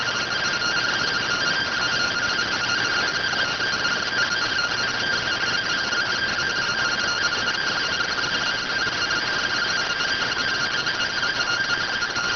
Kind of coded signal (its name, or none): two-tone teleprinter signal